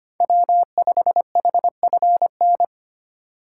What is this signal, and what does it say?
Morse code, heard as a keyed tone, about 25 wpm, 700 Hz. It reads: W5HFN
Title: USA Call Signs - 5 Characters - 8x Character Spacing - 25wpm